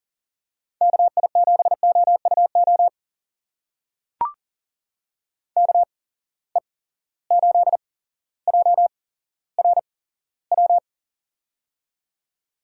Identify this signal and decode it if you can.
Morse code, heard as a keyed tone, about 40 wpm, spaced as for 17 wpm, 700 Hz. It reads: KI7OUO KE8JRW
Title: USA Call Signs - 6 Characters - 8x Character Spacing - 40wpm